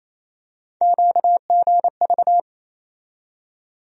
Morse code, heard as a keyed tone, about 27 wpm, 700 Hz. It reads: QGV
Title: USA Call Sign Postfixes - 3 Letters 28wpm